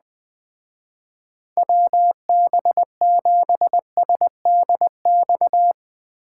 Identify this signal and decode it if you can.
Morse code, heard as a keyed tone, about 20 wpm, 700 Hz. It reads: WB7SDX